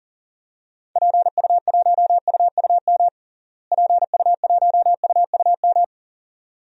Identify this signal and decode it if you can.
Morse code, heard as a keyed tone, about 40 wpm, 700 Hz. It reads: PU1UUM PU1UUM
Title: DXCC-100 Call Signs - 6 Characters - Repeated 2x - 40wpm